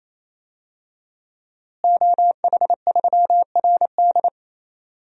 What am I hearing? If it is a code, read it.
Morse code, heard as a keyed tone, about 28 wpm, 700 Hz. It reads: OH3RD
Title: DXCC-100 Call Signs - 5 Characters - Repeated 3x - 28wpm